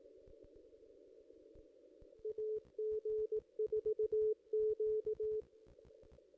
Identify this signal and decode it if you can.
Morse code, heard as a keyed tone, about 16 wpm, 430 Hz. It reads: AG4Q
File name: contests / ss-cw-k5tr-op-n5rz -1700